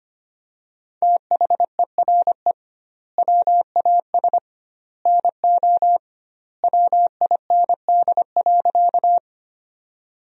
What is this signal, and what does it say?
Morse code, heard as a keyed tone, about 25 wpm, 700 Hz. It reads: THERE WAS NO WIND.
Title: 2x Word-Spacing - Sentences from Top 400 Words - 25wpm